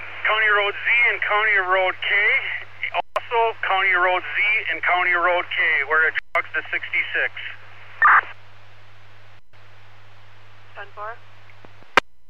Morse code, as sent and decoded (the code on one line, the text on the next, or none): none